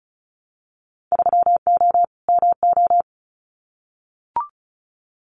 Morse code, heard as a keyed tone, about 36 wpm, 700 Hz. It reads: 3O MO
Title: Winter Field Day Exchanges Encore 35wpm